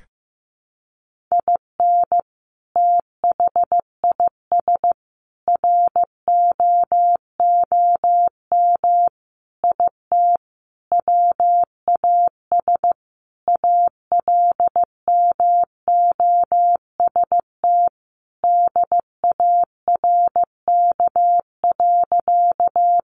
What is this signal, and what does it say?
Morse code, heard as a keyed tone, about 15 wpm, 700 Hz. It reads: IN THIS ROOM IT WAS ALMOST DARK.
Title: Sentences from Top 700 Words 15wpm